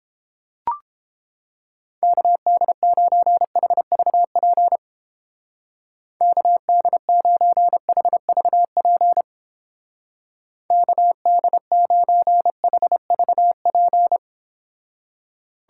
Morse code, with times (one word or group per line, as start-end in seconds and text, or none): 2.0-4.8 KD9HVP
6.2-9.2 KD9HVP
10.7-14.2 KD9HVP